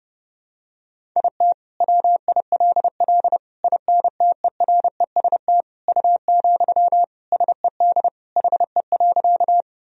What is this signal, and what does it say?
Morse code, heard as a keyed tone, about 32 wpm, 700 Hz. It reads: IT WILL INTEREST U, SED HE.
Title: Sentences from Top 400 Words and Abbreviations 30wpm